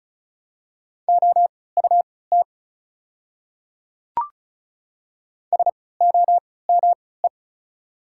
Morse code, heard as a keyed tone, about 35 wpm, spaced as for 20 wpm, 690 Hz. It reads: OUT SOME
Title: 3x Character Spacing - Top 100 Words - 35wpm